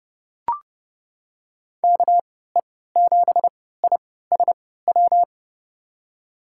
Morse code, heard as a keyed tone, about 30 wpm, spaced as for 17 wpm, 700 Hz. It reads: KE7ISW
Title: USA Call Signs - 3x Character Spacing - 30wpm